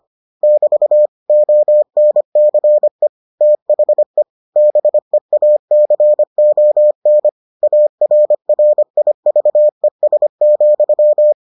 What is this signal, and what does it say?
Morse code, heard as a keyed tone, about 25 wpm, 600 Hz. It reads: <BT> ONCE THE BEACON ARRIVES,